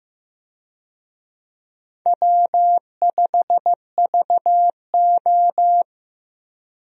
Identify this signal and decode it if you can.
Morse code, heard as a keyed tone, about 15 wpm, 700 Hz. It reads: W5VO
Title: USA Call Signs - 4 Characters - Repeated 2x - 15wpm